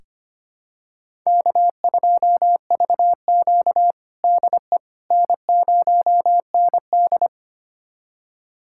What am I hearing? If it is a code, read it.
Morse code, heard as a keyed tone, about 25 wpm, 700 Hz. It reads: K2VQ DE N0ND
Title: USA Callsign DE Callsign - 4 Characters - 25wpm